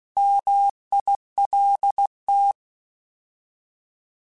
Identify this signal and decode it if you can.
Morse code, heard as a keyed tone, about 16 wpm, 800 Hz. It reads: MILT
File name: Alphabetical / Alphabetical mp3s / Ms @15